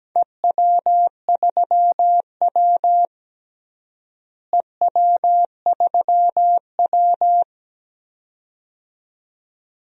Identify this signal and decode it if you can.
Morse code, heard as a keyed tone, about 17 wpm, 700 Hz. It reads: EW3W EW3W
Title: DXCC-100 Call Signs - 4 Characters - Repeated 2x - 17wpm